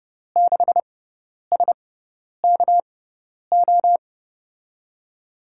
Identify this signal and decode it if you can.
Morse code, heard as a keyed tone, about 30 wpm, spaced as for 10 wpm, 700 Hz. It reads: 6SKO
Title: USA Call Signs - 6x Character Spacing - 30wpm